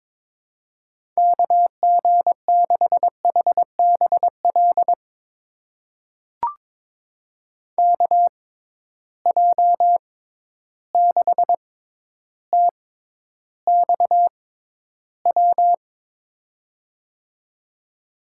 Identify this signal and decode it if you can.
Morse code, heard as a keyed tone, about 22 wpm, spaced as for 11 wpm, 700 Hz. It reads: KG6HBL KJ6TXW
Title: USA Call Signs - 6 Characters - 6x Character Spacing - 22wpm